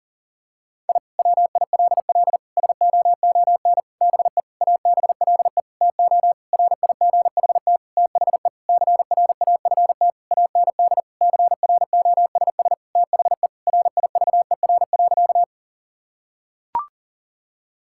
Morse code, heard as a keyed tone, about 40 wpm, 700 Hz. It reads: I WILL SOON BE ABLE TO RIGHT THE CRAFT AND CROSS THE RIVER.
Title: Sentences from Top 1000 Words - Review Part 6 - 40wpm